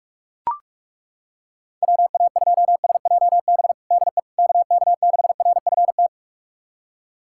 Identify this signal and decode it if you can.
Morse code, heard as a keyed tone, about 41 wpm, 700 Hz. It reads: WA2SJB DE KK6RFT
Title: USA Callsign DE Callsign - 6 Characters - 45wpm